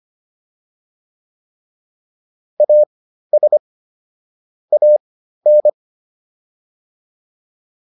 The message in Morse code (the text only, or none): AS AN